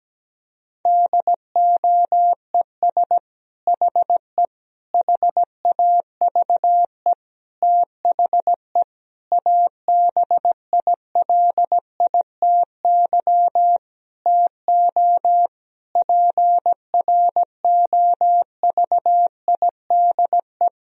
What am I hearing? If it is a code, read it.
Morse code, heard as a keyed tone, about 17 wpm, 700 Hz. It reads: DOES HE HAVE THE ABILITY TO PROVIDE
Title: Sentences from Top 1000 Words - Review Part 3 - 17wpm